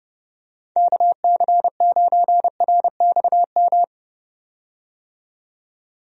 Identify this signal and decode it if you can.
Morse code, heard as a keyed tone, about 30 wpm, 700 Hz. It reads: KC9RXM
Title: USA Call Signs - 6 Characters - 30wpm